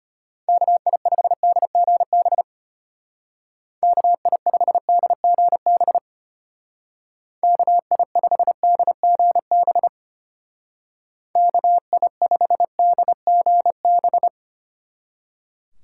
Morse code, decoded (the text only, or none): KI5DGB KI5DGB KI5DGB KI5DGB